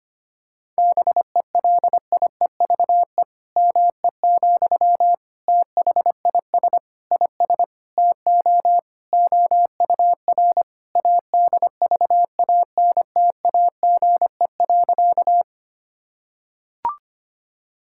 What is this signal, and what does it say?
Morse code, heard as a keyed tone, about 25 wpm, 700 Hz. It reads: BELIEVE ME, THIS IS TO OUR ADVANTAGE.
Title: Sentences from Top 1000 Words - Review Part 6 - 25wpm